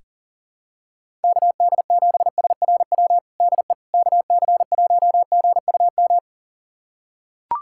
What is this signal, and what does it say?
Morse code, heard as a keyed tone, about 37 wpm, 700 Hz. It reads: KD7SRW DE KC1GUM
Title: USA Callsign DE Callsign - 6 Characters - 40wpm